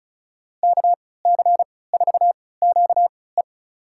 Morse code, heard as a keyed tone, about 36 wpm, spaced as for 20 wpm, 700 Hz. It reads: KC4QE
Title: USA Call Signs - 5 Characters - 3x Character Spacing - 35wpm